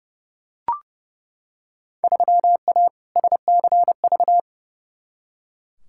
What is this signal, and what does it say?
Morse code, heard as a keyed tone, about 31 wpm, 700 Hz. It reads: 3A SCV
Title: Field Day Exchanges - 30wpm